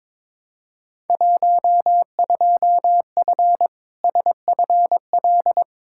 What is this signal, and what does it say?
Morse code, heard as a keyed tone, about 22 wpm, 700 Hz. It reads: 12F SFL
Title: Field Day Exchanges Encore - 22wpm